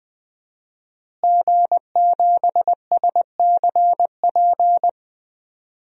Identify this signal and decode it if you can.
Morse code, heard as a keyed tone, about 20 wpm, 700 Hz. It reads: G7SCP
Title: DXCC-100 Call Signs - 5 Characters - Repeated 2x - 20wpm